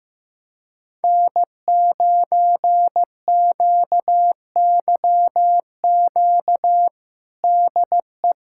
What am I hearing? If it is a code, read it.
Morse code, heard as a keyed tone, about 15 wpm, 700 Hz. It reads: N9QYQ DE
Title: USA Callsign DE Callsign - 5 Characters - 15wpm